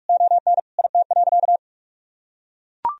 Morse code, heard as a keyed tone, about 42 wpm, 700 Hz. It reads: ON IT.
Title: Sentences from Top 1000 Words - Review Part 4 - 45wpm